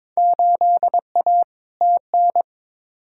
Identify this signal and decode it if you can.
Morse code, heard as a keyed tone, about 22 wpm, 700 Hz. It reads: 8A TN
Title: Field Day Exchanges Encore - 22wpm